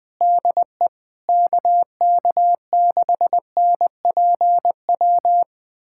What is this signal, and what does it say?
Morse code, heard as a keyed tone, about 20 wpm, 700 Hz. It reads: DE KK6NPW
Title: USA Callsign DE Callsign - 6 Characters - 20wpm